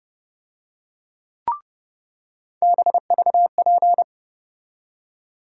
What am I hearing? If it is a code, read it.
Morse code, heard as a keyed tone, about 30 wpm, 700 Hz. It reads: BVP
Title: USA Call Sign Postfixes - 3 Letters 30wpm